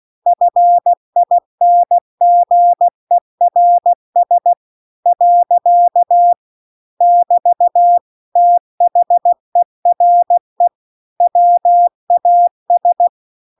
Morse code, heard as a keyed tone, about 16 wpm, 700 Hz. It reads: FINGERS . <BT> THERE WAS
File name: CW-The-Big-Twoheaded-River-2-16WPM